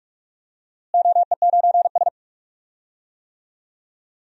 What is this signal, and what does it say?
Morse code, heard as a keyed tone, about 41 wpm, 700 Hz. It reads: OE9S
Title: DXCC-100 Call Signs - 4 Characters - Repeated 2x - 45wpm